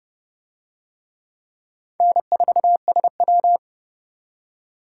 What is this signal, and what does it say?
Morse code, heard as a keyed tone, about 30 wpm, 700 Hz. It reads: N4SW